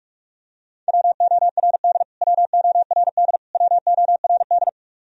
Morse code, heard as a keyed tone, about 43 wpm, 700 Hz. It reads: WORD WORD WORD
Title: Top 100 Words Repeated 45wpm